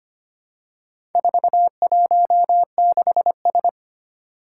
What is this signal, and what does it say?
Morse code, heard as a keyed tone, about 25 wpm, 700 Hz. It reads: 416S
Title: Four Challenging Letter-Number Combinations 25wpm